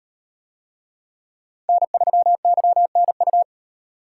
Morse code, heard as a keyed tone, about 38 wpm, 700 Hz. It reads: N3YNU